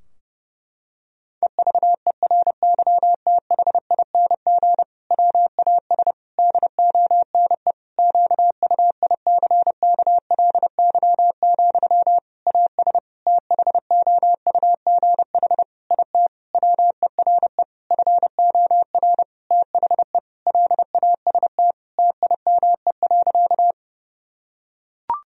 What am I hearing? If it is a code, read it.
Morse code, heard as a keyed tone, about 30 wpm, 700 Hz. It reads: EVERYTHING WAS DONE QUICKLY, AS THOUGH IT WERE FOR THE LAST TIME.
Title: Sentences from Top 1000 Words - Review Part 2 - 30wpm